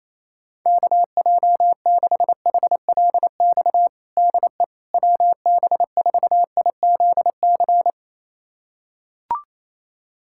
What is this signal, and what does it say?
Morse code, heard as a keyed tone, about 28 wpm, 700 Hz. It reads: KJ6HLX DE WB4IZC
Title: USA Callsign DE Callsign - 6 Characters - 28wpm